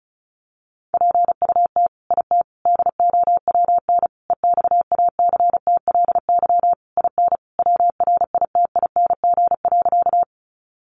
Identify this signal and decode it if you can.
Morse code, heard as a keyed tone, about 34 wpm, 700 Hz. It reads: PUT IT DOWN EXACTLY IN WRITING.
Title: Sentences from Top 900 Words 35wpm